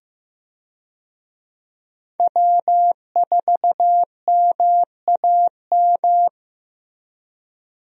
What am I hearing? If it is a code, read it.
Morse code, heard as a keyed tone, about 16 wpm, 700 Hz. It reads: W4MAM
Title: USA Call Signs - 5 Characters - Repeated 2x - 15wpm